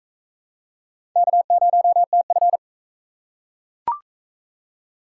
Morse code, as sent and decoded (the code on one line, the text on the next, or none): -.- ----- - ..-.
K0TF